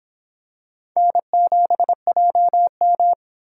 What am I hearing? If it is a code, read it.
Morse code, heard as a keyed tone, about 26 wpm, 700 Hz. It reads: N7JM